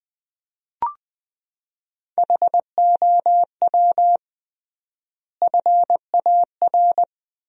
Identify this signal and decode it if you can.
Morse code, heard as a keyed tone, about 20 wpm, 700 Hz. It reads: HOW FAR
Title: 3x Word-Spacing - Sentences from Top 400 Words - 20wpm